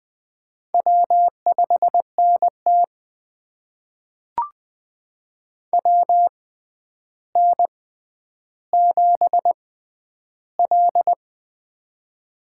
Morse code, heard as a keyed tone, about 20 wpm, spaced as for 10 wpm, 700 Hz. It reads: W5NT WN7L
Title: USA Call Signs - 4 Characters - 6x Character Spacing - 20wpm